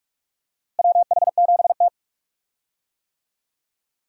Morse code, heard as a keyed tone, about 42 wpm, 700 Hz. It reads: WH7T
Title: USA Call Signs - 4 Characters - Repeated 2x - 45wpm